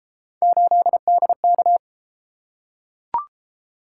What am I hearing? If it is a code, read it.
Morse code, heard as a keyed tone, about 33 wpm, 700 Hz. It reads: 8DK